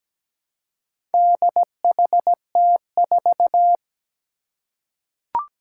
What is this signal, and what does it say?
Morse code, heard as a keyed tone, about 17 wpm, 700 Hz. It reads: DHT4
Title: Four Challenging Letter-Number Combinations 17wpm